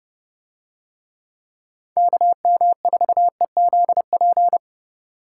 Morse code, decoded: KM4EZP

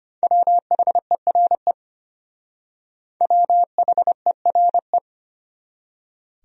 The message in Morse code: WHERE WHERE

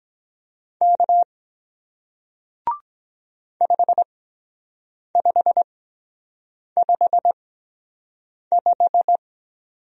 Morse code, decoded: K 5555

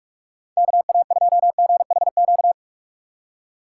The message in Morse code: KA2GHQ

